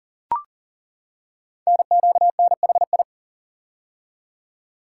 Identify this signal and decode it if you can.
Morse code, heard as a keyed tone, about 40 wpm, 700 Hz. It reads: NQNHI
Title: Code Groups - 5 Letters - 40wpm